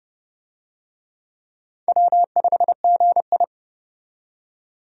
Morse code, heard as a keyed tone, about 30 wpm, 700 Hz. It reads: W5GI
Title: USA Call Signs - 4 Characters - 30wpm